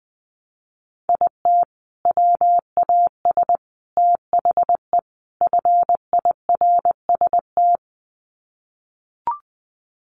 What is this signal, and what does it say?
Morse code, heard as a keyed tone, about 20 wpm, 700 Hz. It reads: IT WAS THE FIRST
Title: Sets of 4 Words 20wpm